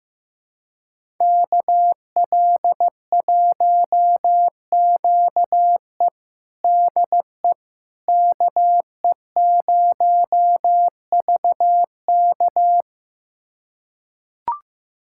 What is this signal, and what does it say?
Morse code, heard as a keyed tone, about 15 wpm, 700 Hz. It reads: KL1QE DE KE0VK
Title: USA Callsign DE Callsign - 5 Characters - 15wpm